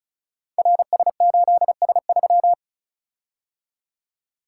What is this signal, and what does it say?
Morse code, heard as a keyed tone, about 36 wpm, 700 Hz. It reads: RS8S3
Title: Code Groups - 5 Letter:Numbers - 35wpm